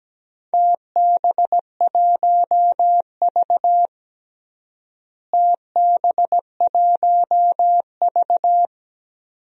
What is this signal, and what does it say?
Morse code, heard as a keyed tone, about 17 wpm, 700 Hz. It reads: TB1V TB1V
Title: DXCC-100 Call Signs - 4 Characters - Repeated 2x - 17wpm